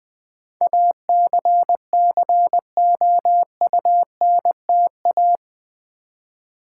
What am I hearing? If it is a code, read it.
Morse code, heard as a keyed tone, about 20 wpm, 700 Hz. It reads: ACCOUNTA
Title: Build A Word - 4 Letters At A Time - 11-Letter Words - 20wpm